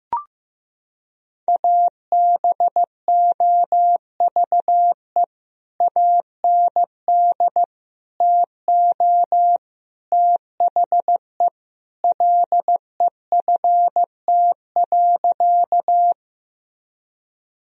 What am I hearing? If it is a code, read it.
Morse code, heard as a keyed tone, about 15 wpm, 700 Hz. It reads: ABOVE AND TO THE LEFT.
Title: Sentences from Top 1000 Words - Review Part 2 - 15wpm